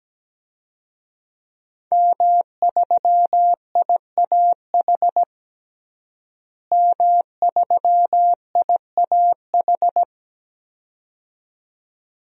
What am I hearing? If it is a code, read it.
Morse code, heard as a keyed tone, about 17 wpm, 700 Hz. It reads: M3IAH M3IAH
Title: DXCC-100 Call Signs - 5 Characters - Repeated 2x - 17wpm